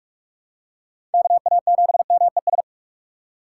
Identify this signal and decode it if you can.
Morse code, heard as a keyed tone, about 41 wpm, 700 Hz. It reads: KA7MES
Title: USA Call Signs - 6 Characters - 6x Character Spacing - 45wpm